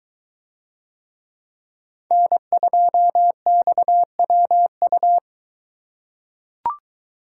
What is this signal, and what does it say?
Morse code, heard as a keyed tone, about 21 wpm, 700 Hz. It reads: N2XWU